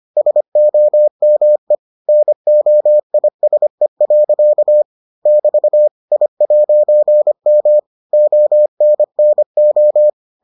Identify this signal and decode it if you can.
Morse code, heard as a keyed tone, about 25 wpm, 600 Hz. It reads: SOME NOISE. <BT> I'M ONNO